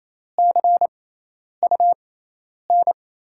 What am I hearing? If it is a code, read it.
Morse code, heard as a keyed tone, about 28 wpm, spaced as for 10 wpm, 700 Hz. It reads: CUN
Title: USA Call Signs - 6x Character Spacing - 28wpm